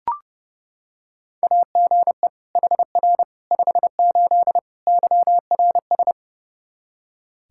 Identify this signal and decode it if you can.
Morse code, heard as a keyed tone, about 30 wpm, 700 Hz. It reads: AGE HR 58 YRS
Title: Practice QSO - Exchange 3 - More Info - 30wpm